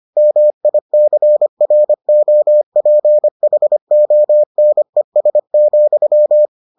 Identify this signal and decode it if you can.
Morse code, heard as a keyed tone, about 25 wpm, 600 Hz. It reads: MICROPHONES,